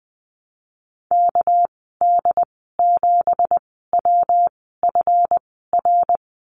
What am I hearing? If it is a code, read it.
Morse code, heard as a keyed tone, about 19 wpm, spaced as for 14 wpm, 700 Hz. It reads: KD7WFR
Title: 207 Lesson 33g - Letter j - Call Signs 20wpm